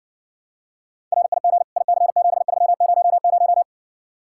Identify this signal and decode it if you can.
Morse code, heard as a keyed tone, about 69 wpm, 700 Hz. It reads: RIG IC7300